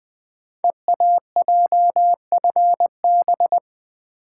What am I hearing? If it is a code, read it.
Morse code, heard as a keyed tone, about 21 wpm, 700 Hz. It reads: EAJFB
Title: Code Groups - 5 Letter:Numbers - 20wpm